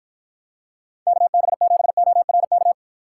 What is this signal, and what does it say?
Morse code, heard as a keyed tone, about 53 wpm, 700 Hz. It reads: KB7ORK